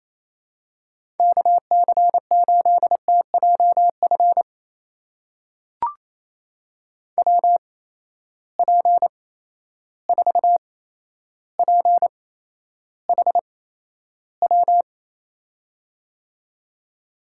Morse code, heard as a keyed tone, about 28 wpm, spaced as for 12 wpm, 700 Hz. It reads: KC8TJF WP4PHW
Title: USA Call Signs - 6 Characters - 8x Character Spacing - 28wpm